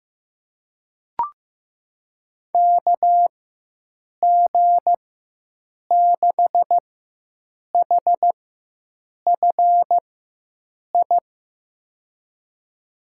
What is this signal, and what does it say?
Morse code, heard as a keyed tone, about 15 wpm, spaced as for 7 wpm, 700 Hz. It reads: KG6HFI
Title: USA Call Signs - 6 Characters - 4x Character Spacing - 15wpm